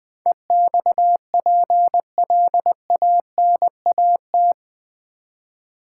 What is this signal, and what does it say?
Morse code, heard as a keyed tone, about 20 wpm, 700 Hz. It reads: EXPLANAT
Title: Build A Word - 4 Letters At A Time - 11-Letter Words - 20wpm